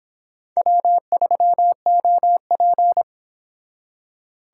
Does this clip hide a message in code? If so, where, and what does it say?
Yes: Morse code from 0.6 to 3.0 s, heard as a keyed tone, about 26 wpm, 700 Hz. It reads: W3OP